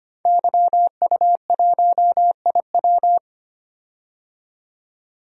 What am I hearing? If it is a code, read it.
Morse code, heard as a keyed tone, about 25 wpm, 700 Hz. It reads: YU1IW
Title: DXCC-100 Call Signs - 5 Characters - 25wpm